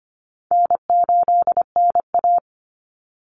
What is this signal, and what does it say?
Morse code, heard as a keyed tone, about 25 wpm, 700 Hz. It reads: N8NA
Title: Call Signs - 25wpm